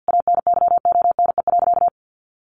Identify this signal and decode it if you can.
Morse code, heard as a keyed tone, about 50 wpm, 700 Hz. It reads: ANYONE.
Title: Sentences from Top 1200 Words 50wpm